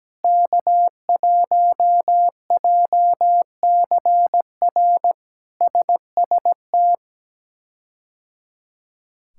Morse code, heard as a keyed tone, about 17 wpm, 700 Hz. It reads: K1JCR SST
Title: K1USN SST QSO Practice - 17wpm